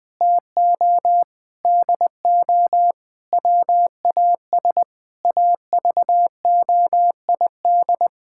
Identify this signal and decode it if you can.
Morse code, heard as a keyed tone, about 20 wpm, 700 Hz. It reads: TO DO WAS AVOID
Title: Sentences from Top 1000 Words - Review Part 6 - 20wpm